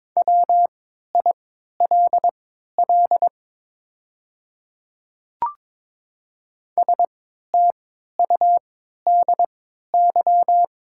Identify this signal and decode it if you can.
Morse code, heard as a keyed tone, about 22 wpm, spaced as for 13 wpm, 700 Hz. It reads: WILL STUDY
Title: 3x Character Spacing - Top 500 Words - 22wpm